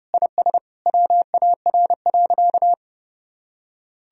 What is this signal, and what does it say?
Morse code, heard as a keyed tone, about 32 wpm, 700 Hz. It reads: IS WAR.
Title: Sentences from Top 1000 Words - Review Part 2 - 30wpm